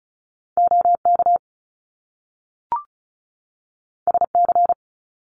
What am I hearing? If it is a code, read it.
Morse code, heard as a keyed tone, about 35 wpm, 700 Hz. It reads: OK SC
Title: US States 35wpm